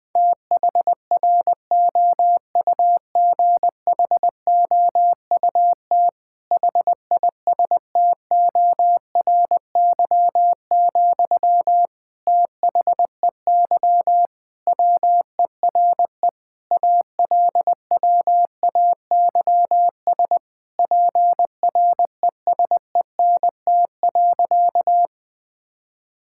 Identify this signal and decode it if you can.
Morse code, heard as a keyed tone, about 19 wpm, 700 Hz. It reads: THROUGHOUT HISTORY, THEY WERE ALWAYS PRESENT.
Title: Sentences from Top 1000 Words - Review Part 5 - 20wpm